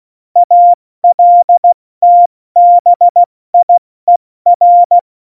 Morse code, heard as a keyed tone, about 16 wpm, 700 Hz. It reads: ALTBIER
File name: Morsekurs / Witze / Witz 08. Oktober 2019, 70 F80